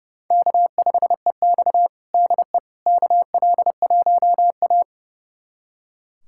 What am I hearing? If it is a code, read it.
Morse code, heard as a keyed tone, about 30 wpm, 700 Hz. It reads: K5EX DE KL1A
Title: USA Callsign DE Callsign - 4 Characters - 30wpm